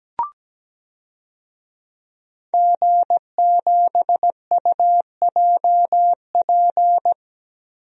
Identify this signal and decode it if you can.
Morse code, heard as a keyed tone, about 17 wpm, 700 Hz. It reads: G7UJP
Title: DXCC-100 Call Signs - 5 Characters - 17wpm